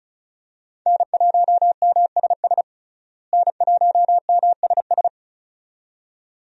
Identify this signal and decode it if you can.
Morse code, heard as a keyed tone, about 34 wpm, 700 Hz. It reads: N1MSS N1MSS
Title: USA Call Signs - 5 Characters - Repeated 2x - 35wpm